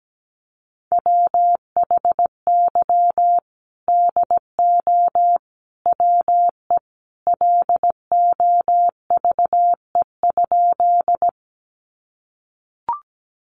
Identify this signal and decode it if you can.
Morse code, heard as a keyed tone, about 17 wpm, 700 Hz. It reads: WHY DO WE LOVE?
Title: Sentences from Top 400 Words 17wpm